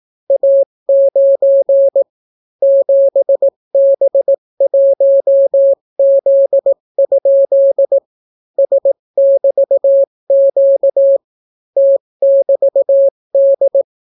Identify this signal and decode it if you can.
Morse code, heard as a keyed tone, about 18 wpm, 550 Hz. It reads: A9 7B1Z? S<BT>Q T<BT>D